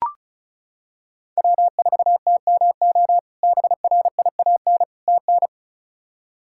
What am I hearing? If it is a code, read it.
Morse code, heard as a keyed tone, about 36 wpm, 700 Hz. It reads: W4TMO BRIAN TN
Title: K1USN SST QSO Practice - 35wpm